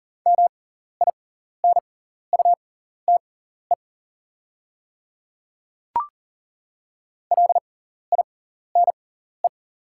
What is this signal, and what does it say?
Morse code, heard as a keyed tone, about 40 wpm, spaced as for 14 wpm, 690 Hz. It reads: MINUTE LINE
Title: 6x Character Spacing - Top 500 Words - 40wpm